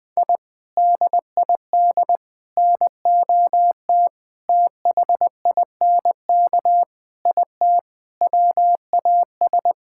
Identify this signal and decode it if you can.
Morse code, heard as a keyed tone, about 20 wpm, 700 Hz. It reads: I DID NOT THINK IT WAS
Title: Sentences with 4-Word Phrases - 20wpm